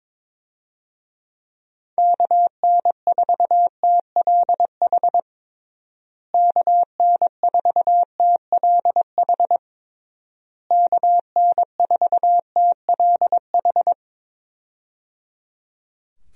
Morse code, heard as a keyed tone, about 22 wpm, 700 Hz. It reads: KN4TLH KN4TLH KN4TLH
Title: USA Call Signs - 6 Characters - Repeated 3x - 22wpm